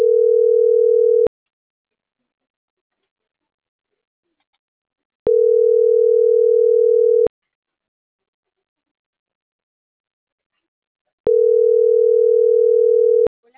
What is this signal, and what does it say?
Ringback tone (North America), 440 + 480 Hz.